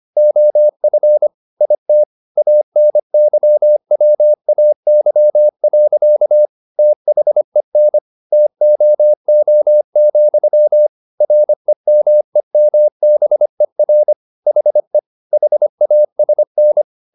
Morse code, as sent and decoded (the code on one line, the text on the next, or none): --- ..-. / .. - / .- -. -.-- .-- .- -.-- .-.-.- / - .... -. / - --- --- --..-- / .-. -- . -- -... .-. / .... / .... .- ... -.
OF IT ANYWAY. THEN TOO, REMEMBER HE HASN